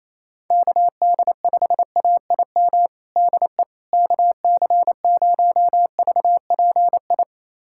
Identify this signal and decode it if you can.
Morse code, heard as a keyed tone, about 28 wpm, 700 Hz. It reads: KD5AIM DE KC0VPI
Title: USA Callsign DE Callsign 28wpm